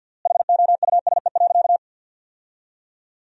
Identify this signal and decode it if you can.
Morse code, heard as a keyed tone, about 50 wpm, 700 Hz. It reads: HOUSE.
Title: 3x Word-Spacing - Sentences from Top 200 Words - 50wpm